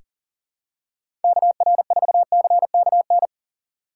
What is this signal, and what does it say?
Morse code, heard as a keyed tone, about 40 wpm, 700 Hz. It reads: KR4CKN
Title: USA Call Signs - 3x Character Spacing - 40wpm